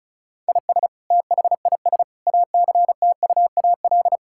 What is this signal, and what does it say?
Morse code, heard as a keyed tone, about 36 wpm, 700 Hz. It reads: IS THIS ACTUAL
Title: Sentences from Top 1000 Words - Review Part 5 - 35wpm